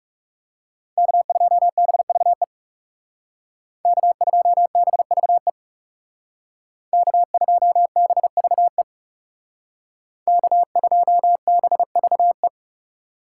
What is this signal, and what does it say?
Morse code, heard as a keyed tone, about 37 wpm, 700 Hz. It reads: K2BVE K2BVE K2BVE K2BVE
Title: Speed Racing - USA 1x3 Call Signs 30wpm